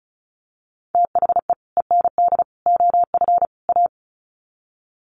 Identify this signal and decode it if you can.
Morse code, heard as a keyed tone, about 34 wpm, 700 Hz. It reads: THE END OF A
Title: Sets of 4 Words 35wpm